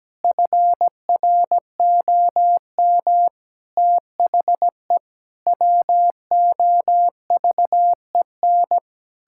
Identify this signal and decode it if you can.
Morse code, heard as a keyed tone, about 17 wpm, 700 Hz. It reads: FROM THE WOVEN
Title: Pangram Challenge v2 - 17wpm